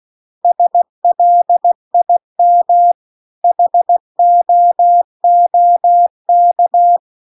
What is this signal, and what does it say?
Morse code, heard as a keyed tone, about 16 wpm, 700 Hz. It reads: SLIM HOOK
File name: CW-The-Big-Twoheaded-River-2-16WPM